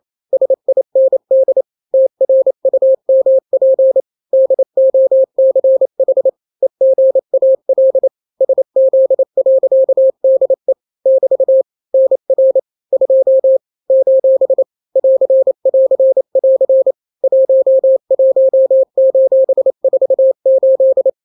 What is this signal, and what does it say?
Morse code, heard as a keyed tone, about 28 wpm, 530 Hz. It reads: SIND TRUMP DOCH EGAL SZ.DE <BT> NR 2 : <AR><AR><AR> 11:48
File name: daily / 27wpm